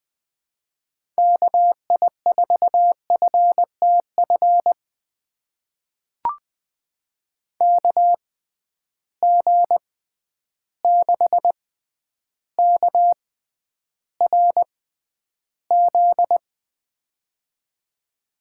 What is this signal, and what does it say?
Morse code, heard as a keyed tone, about 20 wpm, spaced as for 10 wpm, 700 Hz. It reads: KI4FTF KG6KRZ
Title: USA Call Signs - 6 Characters - 6x Character Spacing - 20wpm